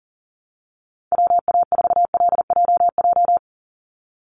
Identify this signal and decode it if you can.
Morse code, heard as a keyed tone, about 40 wpm, 700 Hz. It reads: WA4LJJ
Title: Call Signs - 40wpm